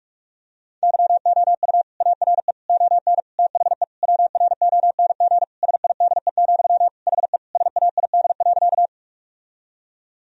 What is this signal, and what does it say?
Morse code, heard as a keyed tone, about 46 wpm, 700 Hz. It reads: YOU ARE ON THE WRONG SIDE, HE SAID.
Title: Sentences from Top 1000 Words - Review Part 1 - 45wpm